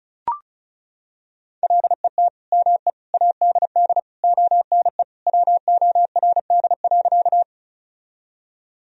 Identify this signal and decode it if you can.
Morse code, heard as a keyed tone, about 34 wpm, 700 Hz. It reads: LET ME ADD ONE WORD.
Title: Sentences from Top 1000 Words - Review Part 1 - 35wpm